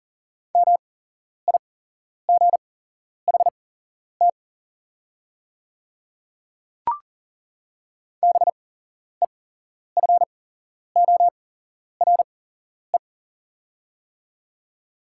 Morse code, heard as a keyed tone, about 41 wpm, spaced as for 11 wpm, 690 Hz. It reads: MIGHT BEFORE
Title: 8x Character Spacing - Top 500 Words - 40wpm